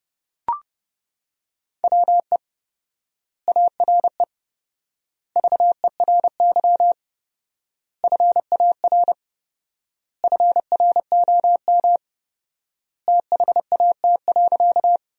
Morse code, heard as a keyed tone, about 30 wpm, 700 Hz. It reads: WE ARE VERY FAR FROM THAT.
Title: 4x Word-Spacing - Sentences from Top 300 Words - 30wpm